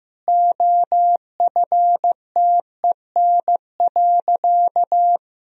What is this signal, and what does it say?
Morse code, heard as a keyed tone, about 16 wpm, 700 Hz. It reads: OFTEN.
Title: Sentences from Top 1000 Words - Review Part 3 - 15wpm